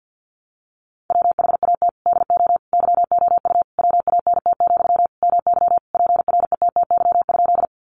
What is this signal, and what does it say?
Morse code, heard as a keyed tone, about 50 wpm, 700 Hz. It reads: WHAT DO YOU WANT, MY PRETTY?
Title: Sentences from Top 1000 Words 50wpm